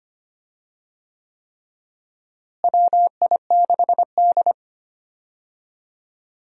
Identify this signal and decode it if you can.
Morse code, heard as a keyed tone, about 25 wpm, 700 Hz. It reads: WI6D
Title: USA Call Signs - 4 Characters - Repeated 3x - 25wpm